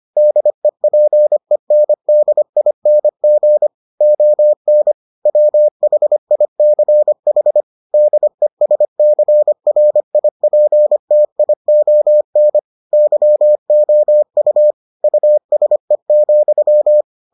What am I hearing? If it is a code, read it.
Morse code, heard as a keyed tone, about 25 wpm, 600 Hz. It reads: DEPENDING ON WHICH DESCRIPTION YOU USE,